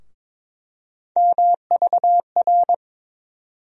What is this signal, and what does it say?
Morse code, heard as a keyed tone, about 22 wpm, 700 Hz. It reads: MVR
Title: USA Call Sign Postfixes - 3 Letters 22wpm